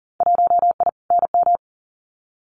Morse code, heard as a keyed tone, about 40 wpm, 700 Hz. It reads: 1I NM
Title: Winter Field Day Exchanges Encore 40wpm